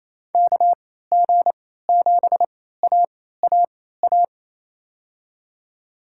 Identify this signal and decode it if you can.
Morse code, heard as a keyed tone, about 28 wpm, spaced as for 16 wpm, 700 Hz. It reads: KG7AAA
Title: USA Call Signs - 3x Character Spacing - 28wpm